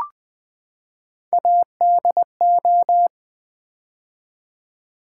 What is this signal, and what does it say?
Morse code, heard as a keyed tone, about 20 wpm, 700 Hz. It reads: ADO
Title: Build A Word - 3 Letters At A Time - 8-Letter Words - 20wpm